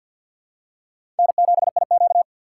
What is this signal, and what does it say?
Morse code, heard as a keyed tone, about 47 wpm, 700 Hz. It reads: N7IQ
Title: USA Call Signs - 3x Character Spacing - 50wpm